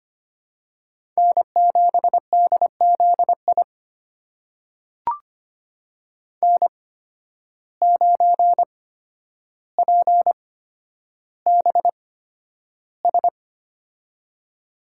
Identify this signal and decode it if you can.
Morse code, heard as a keyed tone, about 25 wpm, spaced as for 11 wpm, 700 Hz. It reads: N7DZI N9PBS
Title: USA Call Signs - 5 Characters - 8x Character Spacing - 25wpm